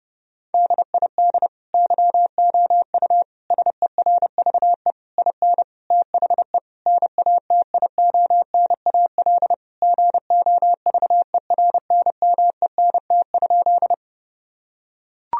Morse code, heard as a keyed tone, about 30 wpm, 700 Hz. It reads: DID YOU SERVE IN THE NATIONAL GOVERNMENT?
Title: Sentences from Top 1000 Words - Review Part 4 - 30wpm